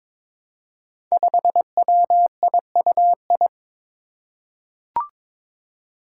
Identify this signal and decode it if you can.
Morse code, heard as a keyed tone, about 22 wpm, 700 Hz. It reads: 5WIUI
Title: Five Challenging Letter-Number Combinations 22wpm